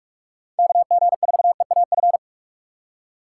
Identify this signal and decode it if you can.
Morse code, heard as a keyed tone, about 44 wpm, 700 Hz. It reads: KG4EAF